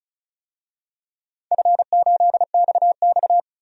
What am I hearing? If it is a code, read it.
Morse code, heard as a keyed tone, about 36 wpm, 700 Hz. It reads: F8XX
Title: DXCC-100 Call Signs - 4 Characters - 35wpm